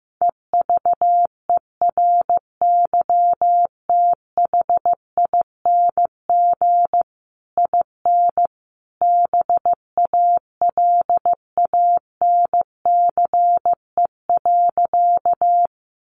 Morse code, heard as a keyed tone, about 15 wpm, 700 Hz. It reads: EVERYTHING IN BALANCE.
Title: A Sound Of Thunder 15wpm